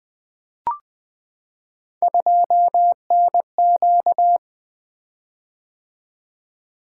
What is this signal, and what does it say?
Morse code, heard as a keyed tone, about 20 wpm, 700 Hz. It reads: 2NQ
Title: Three Letter-Number Combinations - 20wpm